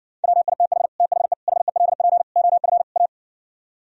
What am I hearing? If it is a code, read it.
Morse code, heard as a keyed tone, about 59 wpm, 700 Hz. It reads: WITH THE HELP OF A